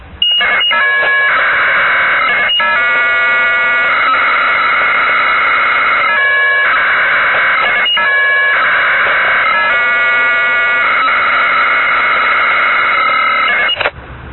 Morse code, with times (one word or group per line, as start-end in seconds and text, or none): none